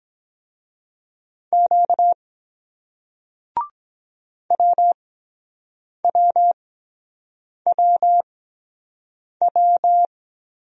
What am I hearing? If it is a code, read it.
Morse code, heard as a keyed tone, about 24 wpm, spaced as for 7 wpm, 700 Hz. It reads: Q WWWW